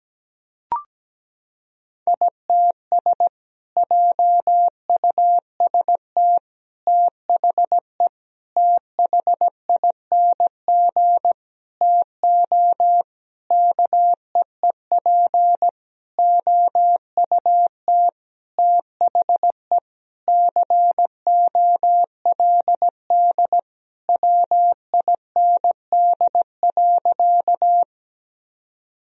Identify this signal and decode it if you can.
Morse code, heard as a keyed tone, about 17 wpm, 700 Hz. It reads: ITS JUST THE THING TO KEEP OUT THE COLD WIND.
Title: Sentences from Top 1000 Words - Review Part 2 - 17wpm